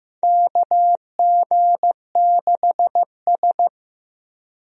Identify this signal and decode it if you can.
Morse code, heard as a keyed tone, about 16 wpm, 700 Hz. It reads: KG6S
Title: USA Call Signs - 4 Characters - Repeated 3x - 15wpm